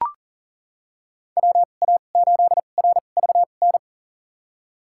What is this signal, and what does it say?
Morse code, heard as a keyed tone, about 40 wpm, spaced as for 29 wpm, 700 Hz. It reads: WA8RVN
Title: USA Call Signs - 6 Characters - 2x Character Spacing - 40wpm